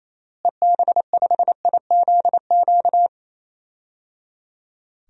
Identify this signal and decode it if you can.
Morse code, heard as a keyed tone, about 28 wpm, 700 Hz. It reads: EB5IZQ
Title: DXCC-100 Call Signs - 6 Characters - Repeated 3x - 28wpm